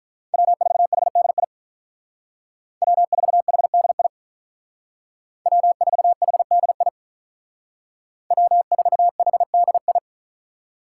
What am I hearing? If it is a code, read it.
Morse code, heard as a keyed tone, about 45 wpm, 700 Hz. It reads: W4HDI W4HDI W4HDI W4HDI